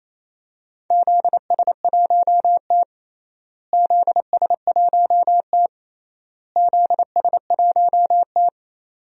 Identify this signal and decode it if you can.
Morse code, heard as a keyed tone, about 28 wpm, 700 Hz. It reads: ZS1T ZS1T ZS1T
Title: DXCC-100 Call Signs - 4 Characters - Repeated 3x - 28wpm